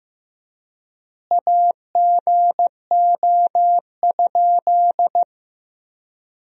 Morse code, heard as a keyed tone, about 15 wpm, 700 Hz. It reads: AGO?
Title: 3x Word-Spacing - Sentences from Top 500 Words - 15wpm